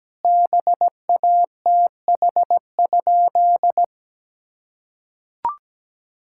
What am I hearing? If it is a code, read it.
Morse code, heard as a keyed tone, about 17 wpm, 700 Hz. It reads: BATH?
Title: Sentences from Top 1600 Words 17wpm